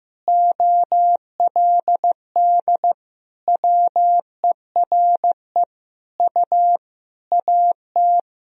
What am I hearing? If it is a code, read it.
Morse code, heard as a keyed tone, about 15 wpm, 700 Hz. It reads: OLD WERE U AT
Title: Sentences from Top 200 Words and Abbreviations 15wpm